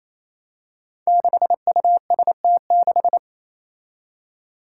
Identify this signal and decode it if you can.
Morse code, heard as a keyed tone, about 28 wpm, 700 Hz. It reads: 6UST6
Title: Five Challenging Letter-Number Combinations 28wpm